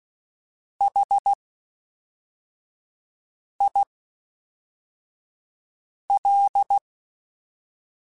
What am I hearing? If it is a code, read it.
Morse code, heard as a keyed tone, about 16 wpm, spaced as for 4 wpm, 800 Hz. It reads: HIL
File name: H Prefixes / H mp3s